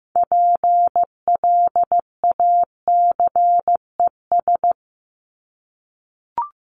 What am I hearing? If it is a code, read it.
Morse code, heard as a keyed tone, about 15 wpm, 700 Hz. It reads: PLACES
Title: ES:S Suffix - Top 100 Words with Base Words 15wpm